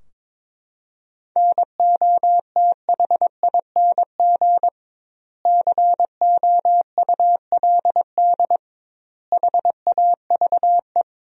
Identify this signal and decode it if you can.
Morse code, heard as a keyed tone, about 21 wpm, 700 Hz. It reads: NOTHING COULD HAVE
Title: 2x Word-Spacing - Sentences from Top 500 Words - 22wpm